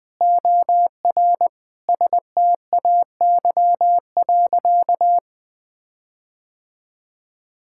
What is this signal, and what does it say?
Morse code, heard as a keyed tone, about 21 wpm, 700 Hz. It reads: OR STAY.
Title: Sentences with 5-Word Phrases - 20wpm